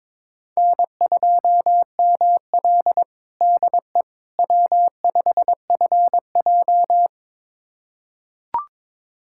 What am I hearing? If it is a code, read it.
Morse code, heard as a keyed tone, about 21 wpm, 700 Hz. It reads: N2ML DE W5FJ
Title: USA Callsign DE Callsign - 4 Characters - 22wpm